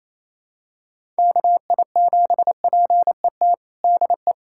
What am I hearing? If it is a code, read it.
Morse code, heard as a keyed tone, about 28 wpm, 700 Hz. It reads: KI7PET DE
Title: USA Callsign DE Callsign - 6 Characters - 28wpm